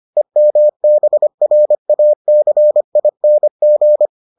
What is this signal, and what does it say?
Morse code, heard as a keyed tone, about 25 wpm, 600 Hz. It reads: EMBRACING